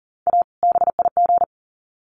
Morse code, heard as a keyed tone, about 40 wpm, 700 Hz. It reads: A BIG
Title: Sets of 2 Words 40wpm